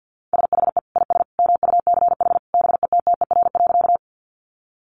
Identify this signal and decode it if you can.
Morse code, heard as a keyed tone, about 50 wpm, 700 Hz. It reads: SHE IS MUCH BETTER.
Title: Sentences from Top 400 Words 50wpm